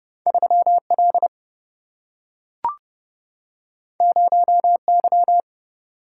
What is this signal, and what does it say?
Morse code, heard as a keyed tone, about 30 wpm, 700 Hz. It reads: 3 L 0 Y
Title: Single-Letter Single-Number 30wpm